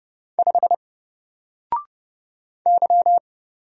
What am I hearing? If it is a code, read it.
Morse code, heard as a keyed tone, about 30 wpm, 700 Hz. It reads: 5 Y
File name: Single-Letter-Number-30wpm